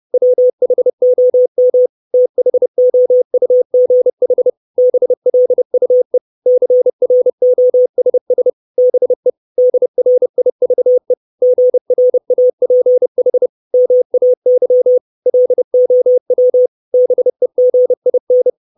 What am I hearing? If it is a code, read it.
Morse code, heard as a keyed tone, about 30 wpm, 500 Hz. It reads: WHOM THOUGH BLUE CROSS BE DRIVE GRAPH MAY LOW BEGIN
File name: 632 Words 30WPM